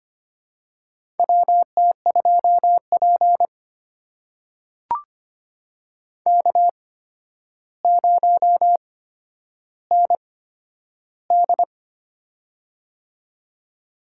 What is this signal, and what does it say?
Morse code, heard as a keyed tone, about 25 wpm, spaced as for 11 wpm, 700 Hz. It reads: WT2P K0ND
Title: USA Call Signs - 4 Characters - 8x Character Spacing - 25wpm